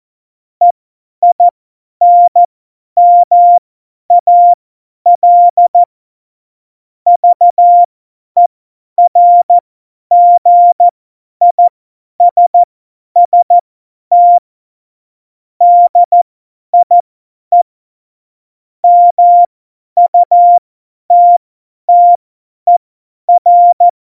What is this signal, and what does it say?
Morse code, heard as a keyed tone, about 14 wpm, spaced as for 10 wpm, 700 Hz. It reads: EINMAL VERGISST DIE MUTTER
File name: Morsekurs / Witze / Witz 03. Oktober 17, Tempo 50 F70